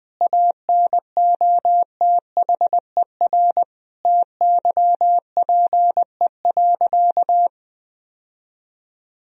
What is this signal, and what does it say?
Morse code, heard as a keyed tone, about 20 wpm, 700 Hz. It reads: ANOTHER TYPE.
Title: Sentences from Top 1000 Words - Review Part 3 - 20wpm